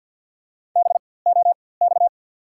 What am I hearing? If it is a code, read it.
Morse code, heard as a keyed tone, about 48 wpm, spaced as for 23 wpm, 690 Hz. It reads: BOX
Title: 4x Character Spacing - Top 500 Words - 50wpm